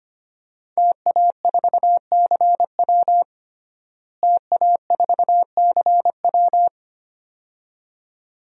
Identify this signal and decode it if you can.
Morse code, heard as a keyed tone, about 25 wpm, 700 Hz. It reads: TA4CW TA4CW
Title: DXCC-100 Call Signs - 5 Characters - Repeated 2x - 25wpm